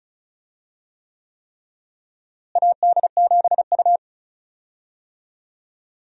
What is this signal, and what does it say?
Morse code, heard as a keyed tone, about 36 wpm, 700 Hz. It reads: AD7U
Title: USA Call Signs - 4 Characters - Repeated 3x - 35wpm